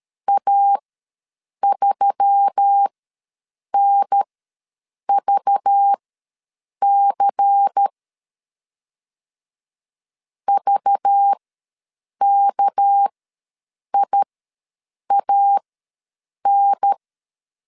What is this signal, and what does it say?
Morse code, heard as a keyed tone, about 13 wpm, spaced as for 7 wpm, 800 Hz. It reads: A3NVC VKIAN
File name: LEZIONI / 6 PKX0 / 6 PK ws 12wpm 3116